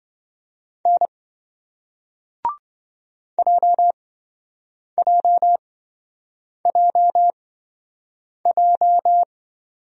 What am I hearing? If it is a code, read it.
Morse code, heard as a keyed tone, about 26 wpm, spaced as for 7 wpm, 700 Hz. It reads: N JJJJ